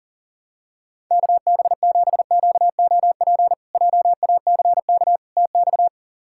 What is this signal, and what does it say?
Morse code, heard as a keyed tone, about 40 wpm, 700 Hz. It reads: KB7QOP JACK TX
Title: K1USN SST QSO Practice - 40wpm